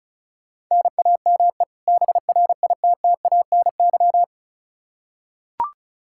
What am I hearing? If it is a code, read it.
Morse code, heard as a keyed tone, about 36 wpm, 700 Hz. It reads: NAME BRITTANY
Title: K1USN SST QSO Practice 35wpm